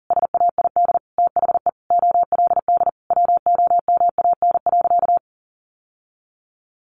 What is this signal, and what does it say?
Morse code, heard as a keyed tone, about 40 wpm, 700 Hz. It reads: SAID THE OLD WOMAN.
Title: Sentences from Top 1100 Words 40wpm